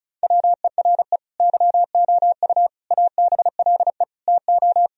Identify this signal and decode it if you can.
Morse code, heard as a keyed tone, about 34 wpm, 700 Hz. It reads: WERE YOU ABLE TO
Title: Sentences from Top 1000 Words - Review Part 2 - 35wpm